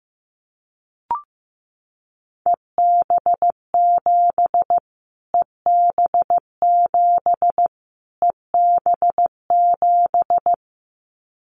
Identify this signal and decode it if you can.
Morse code, heard as a keyed tone, about 15 wpm, 700 Hz. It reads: EB7 EB7 EB7
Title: Callsign Prefixes for Top 25 DXCC - Repeated 3x - 15wpm